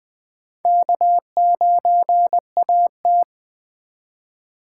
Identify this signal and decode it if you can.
Morse code, heard as a keyed tone, about 21 wpm, 700 Hz. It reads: K9AT
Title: Speed Racing - USA 1x2 Call Signs 17wpm